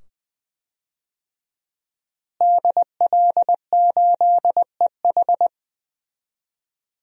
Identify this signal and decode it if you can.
Morse code, heard as a keyed tone, about 20 wpm, 700 Hz. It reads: DL8EH
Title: DXCC-100 Call Signs - 5 Characters - Repeated 2x - 20wpm